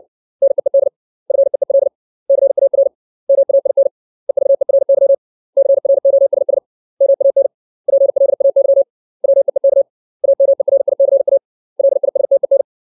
Morse code, heard as a keyed tone, about 60 wpm, 550 Hz. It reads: NEED FEEL CAR MAIN EVERY CROSS MAN PLAY WEEK AMERICA LISTEN
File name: 632 words / 632 Words 60WPM